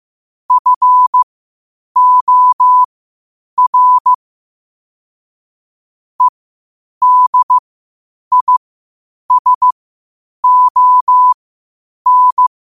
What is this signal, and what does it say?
Morse code, heard as a keyed tone, about 15 wpm, spaced as for 8 wpm, 1000 Hz. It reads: FOR EDISON
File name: qotdfiles / 8wpm